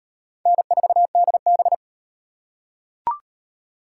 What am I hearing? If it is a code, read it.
Morse code, heard as a keyed tone, about 38 wpm, 700 Hz. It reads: N4DB